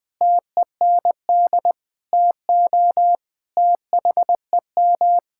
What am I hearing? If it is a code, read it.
Morse code, heard as a keyed tone, about 20 wpm, 700 Hz. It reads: TEND TO THEM
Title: Sentences from Top 1000 Words - Review Part 5 - 20wpm